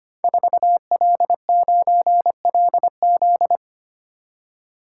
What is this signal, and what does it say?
Morse code, heard as a keyed tone, about 25 wpm, 700 Hz. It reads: 4L9LZ
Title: Code Groups - 5 Letter:Numbers - 25wpm